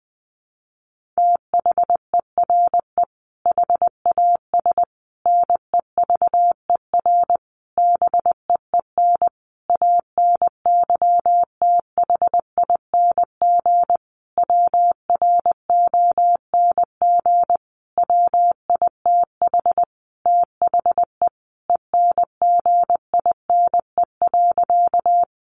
Morse code, heard as a keyed tone, about 20 wpm, 700 Hz. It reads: THERE HAS NEVER BEEN ANYTHING WRONG WITH THE ENGINE.
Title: Sentences from Top 1500 Words 20wpm